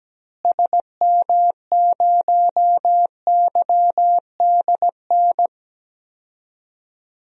Morse code, heard as a keyed tone, about 17 wpm, 700 Hz. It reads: SM0YDN
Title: DXCC-100 Call Signs - 6 Characters - Repeated 2x - 17wpm